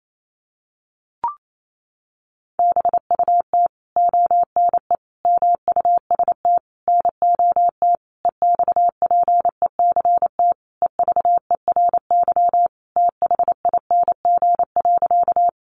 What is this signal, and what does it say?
Morse code, heard as a keyed tone, about 28 wpm, 700 Hz. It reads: BUT ONE MUST NOT EXPECT EVERYTHING.
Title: Sentences from Top 1100 Words 28wpm